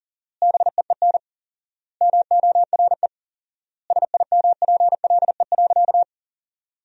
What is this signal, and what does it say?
Morse code, heard as a keyed tone, about 40 wpm, 700 Hz. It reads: BEEN MORE SIMPLE.
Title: 4x Word-Spacing - Sentences from Top 400 Words - 40wpm